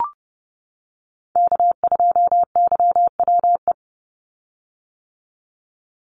Morse code, heard as a keyed tone, about 30 wpm, 700 Hz. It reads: K2YWE
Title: Call Signs - 30wpm